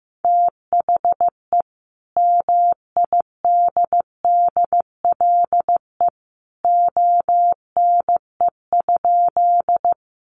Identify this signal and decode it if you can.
Morse code, heard as a keyed tone, about 15 wpm, 700 Hz. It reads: THE MIDDLE ONE?
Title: Sentences from Top 900 Words 15wpm